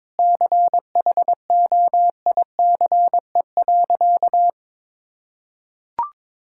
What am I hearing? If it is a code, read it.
Morse code, heard as a keyed tone, about 22 wpm, 700 Hz. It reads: CHOICE.
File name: practice / 22wpm / Sentences with 8-Word Phrases - 22wpm